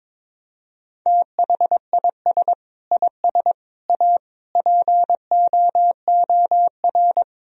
Morse code, heard as a keyed tone, about 21 wpm, 700 Hz. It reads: THIS IS A POOR
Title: Sentences from Top 1000 Words - Review Part 4 - 22wpm